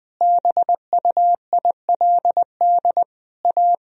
Morse code, heard as a keyed tone, about 20 wpm, 700 Hz. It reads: BUILD A
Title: Sentences from Top 1000 Words - Review Part 1 - 20wpm